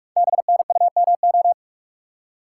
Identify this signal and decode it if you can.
Morse code, heard as a keyed tone, about 44 wpm, 700 Hz. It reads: BNUMO
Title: Code Groups - 5 Letters - 45wpm